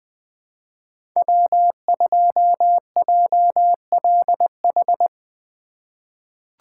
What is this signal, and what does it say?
Morse code, heard as a keyed tone, about 20 wpm, 700 Hz. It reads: W2JLH
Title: USA Call Signs - 5 Characters - 3x Character Spacing - 20wpm